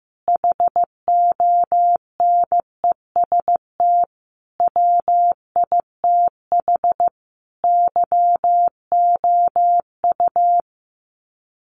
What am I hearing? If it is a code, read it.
Morse code, heard as a keyed tone, about 15 wpm, 700 Hz. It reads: HONEST WITH YOU
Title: Sets of 5 Words Encore 15wpm